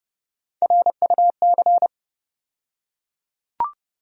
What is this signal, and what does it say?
Morse code, heard as a keyed tone, about 30 wpm, 700 Hz. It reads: RUC
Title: USA Call Sign Postfixes - 3 Letters 30wpm